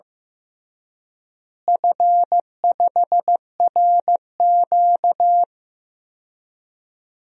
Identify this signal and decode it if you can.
Morse code, heard as a keyed tone, about 15 wpm, 700 Hz. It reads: F5RQ
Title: DXCC-100 Call Signs - 4 Characters - Repeated 3x - 15wpm